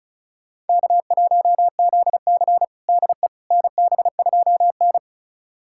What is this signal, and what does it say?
Morse code, heard as a keyed tone, about 36 wpm, 700 Hz. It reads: K1ZC DE NB2N
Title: USA Callsign DE Callsign - 4 Characters - 35wpm